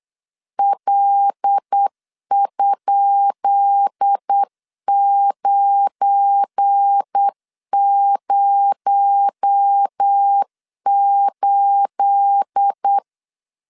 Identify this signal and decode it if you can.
Morse code, heard as a keyed tone, about 8 wpm, 800 Hz. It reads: L?908
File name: LEZIONI / 10 SIGNS / 10B INT 8wpm 3534